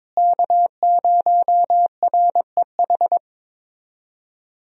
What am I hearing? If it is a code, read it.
Morse code, heard as a keyed tone, about 22 wpm, 700 Hz. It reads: K0REH
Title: USA Call Signs - 5 Characters - Repeated 2x - 22wpm